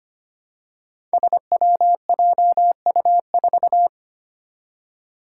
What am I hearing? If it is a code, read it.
Morse code, heard as a keyed tone, about 25 wpm, 700 Hz. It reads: SWJU4
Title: Five Challenging Letter-Number Combinations 25wpm